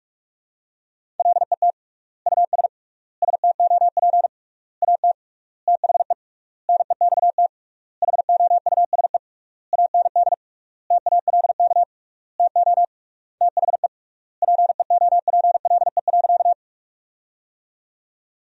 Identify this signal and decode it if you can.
Morse code, heard as a keyed tone, about 48 wpm, 700 Hz. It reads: LET US STOP AT THE NEXT HOUSE AND TALK TO THE PEOPLE.
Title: 3x Word-Spacing - Sentences from Top 400 Words - 45wpm